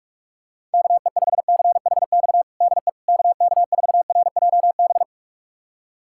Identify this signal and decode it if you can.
Morse code, heard as a keyed tone, about 46 wpm, 700 Hz. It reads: KE5CHX DE KK4RJB
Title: USA Callsign DE Callsign - 6 Characters - 45wpm